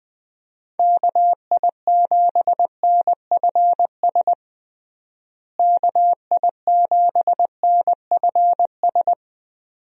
Morse code, heard as a keyed tone, about 21 wpm, 700 Hz. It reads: KI7NFS KI7NFS
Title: USA Call Signs - 6 Characters - Repeated 2x - 20wpm